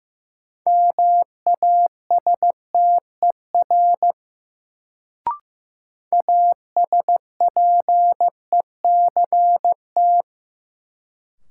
Morse code, heard as a keyed tone, about 15 wpm, 700 Hz. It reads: MASTER ASPECT
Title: Sentences from Top 1300 Words - New Word Drill Alt - 15wpm